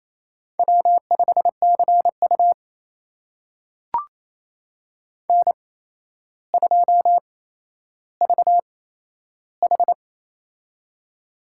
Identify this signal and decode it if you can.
Morse code, heard as a keyed tone, about 27 wpm, spaced as for 12 wpm, 700 Hz. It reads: W5CU N2VH
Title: USA Call Signs - 4 Characters - 8x Character Spacing - 28wpm